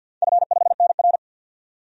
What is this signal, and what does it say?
Morse code, heard as a keyed tone, about 46 wpm, 700 Hz. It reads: F5NR
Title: DXCC-100 Call Signs - 4 Characters - 50wpm